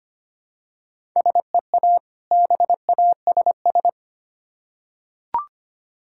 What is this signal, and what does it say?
Morse code, heard as a keyed tone, about 25 wpm, 700 Hz. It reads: SEA BASS
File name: practice / 25wpm / Difficult Words with Hints - 25wpm